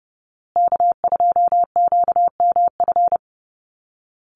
Morse code, heard as a keyed tone, about 30 wpm, 700 Hz. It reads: K2QMF
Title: Call Signs Repeated 2x - 30wpm